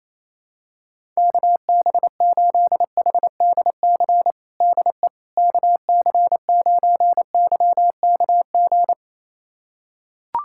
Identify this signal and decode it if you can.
Morse code, heard as a keyed tone, about 28 wpm, 700 Hz. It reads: KB8HDC DE KC9YKG
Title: USA Callsign DE Callsign - 6 Characters - 28wpm